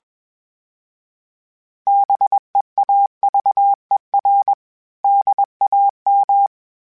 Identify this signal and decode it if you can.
Morse code, heard as a keyed tone, about 21 wpm, 800 Hz. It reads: BEAVER DAM
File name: fromSMs / FromSMs mp3s / Cities@20